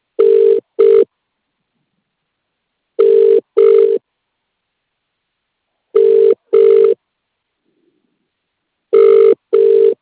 Ringback tone (UK), 400 + 450 Hz.